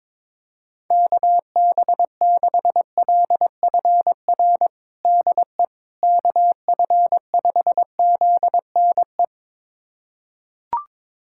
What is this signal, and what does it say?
Morse code, heard as a keyed tone, about 22 wpm, 700 Hz. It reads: KB6LFR DE KF5ZNE
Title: USA Callsign DE Callsign 22wpm